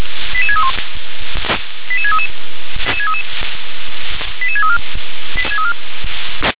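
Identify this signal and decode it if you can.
Five-tone selective call (ZVEI), heard as quick runs of five tones, tones of about 70 ms. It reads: 88631 88633 88634
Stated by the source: ZVEI five-tone selective call received on an SDR.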